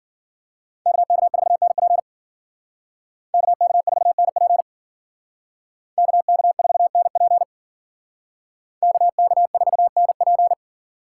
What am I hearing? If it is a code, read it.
Morse code, heard as a keyed tone, about 52 wpm, 700 Hz. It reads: KK4NP KK4NP KK4NP KK4NP